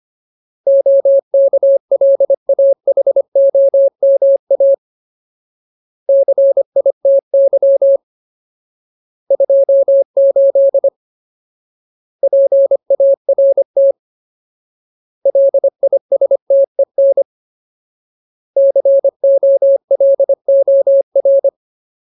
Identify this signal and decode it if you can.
Morse code, heard as a keyed tone, about 25 wpm, 550 Hz. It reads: OKLAHOMA CITY 28 PART LISTEN COLOR
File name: Ultimate Word List / 25WPM Ultimate